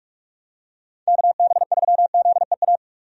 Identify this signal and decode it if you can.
Morse code, heard as a keyed tone, about 46 wpm, 700 Hz. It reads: KB3ZEA
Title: USA Call Signs - 6 Characters - 45wpm